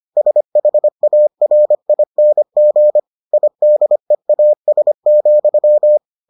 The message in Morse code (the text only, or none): SHARING IDEAS,